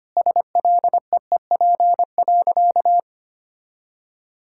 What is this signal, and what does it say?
Morse code, heard as a keyed tone, about 25 wpm, 700 Hz. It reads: SLEEP.
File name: practice / 25wpm / Sentences with 2-Word Phrases - 25wpm